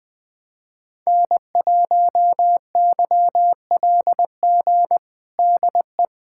Morse code, heard as a keyed tone, about 21 wpm, 700 Hz. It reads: N1YLG DE
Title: USA Callsign DE Callsign - 5 Characters - 20wpm